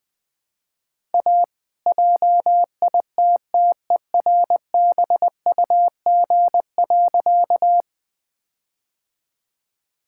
Morse code, heard as a keyed tone, about 20 wpm, 700 Hz. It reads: A JITTERBUG.
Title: All-Age Appropriate Jokes - 20wpm